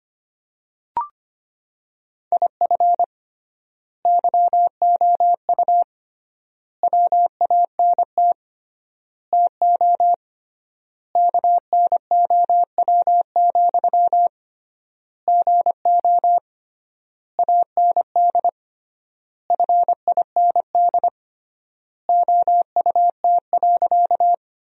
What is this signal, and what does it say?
Morse code, heard as a keyed tone, about 25 wpm, 700 Hz. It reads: IF YOU WANT TO KNOW, GO AND FIND OUT.
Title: 3x Word-Spacing - Sentences from Top 200 Words - 25wpm